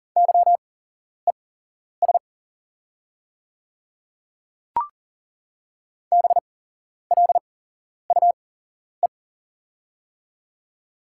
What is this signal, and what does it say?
Morse code, heard as a keyed tone, about 40 wpm, spaced as for 11 wpm, 700 Hz. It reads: YES BLUE
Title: 8x Character Spacing - Top 500 Words - 40wpm